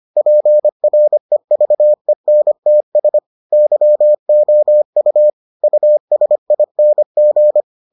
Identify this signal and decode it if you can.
Morse code, heard as a keyed tone, about 25 wpm, 600 Hz. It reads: PREVENTS YOU USING